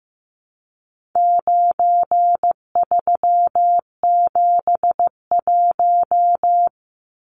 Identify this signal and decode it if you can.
Morse code, heard as a keyed tone, about 15 wpm, 700 Hz. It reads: 9371
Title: Numbers - 4-Digit 15wpm